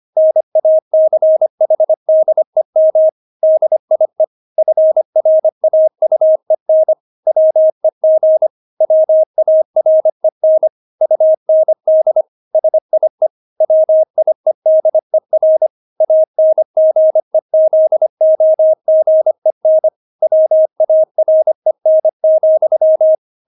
Morse code, heard as a keyed tone, about 25 wpm, 630 Hz. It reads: NACHDEM DIE FRAUEN WEG WAREN UND SIE WIEDER ANGEZOGEN WAREN,
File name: Witz6 25wpm